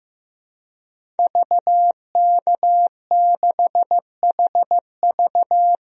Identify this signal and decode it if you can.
Morse code, heard as a keyed tone, about 15 wpm, 700 Hz. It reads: VK6HV
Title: DXCC-100 Call Signs - 5 Characters - 15wpm